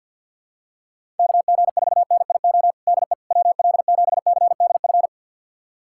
Morse code, heard as a keyed tone, about 52 wpm, 700 Hz. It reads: KG4NIO DE WL7CDF
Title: USA Callsign DE Callsign - 6 Characters - 50wpm